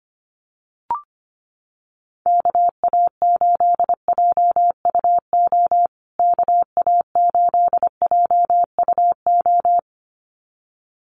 Morse code, heard as a keyed tone, about 25 wpm, 700 Hz. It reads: KA8JUO KA8JUO
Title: USA Call Signs Repeated 2x 25wpm